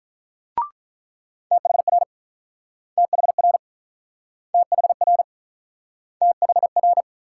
Morse code, heard as a keyed tone, about 49 wpm, 700 Hz. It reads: THR THR THR THR